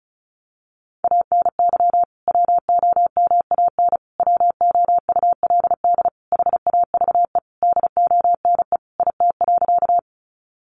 Morse code, heard as a keyed tone, about 34 wpm, 700 Hz. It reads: ANY WOMAN WOULD HAVE DONE IT.
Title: Sentences from Top 1100 Words 35wpm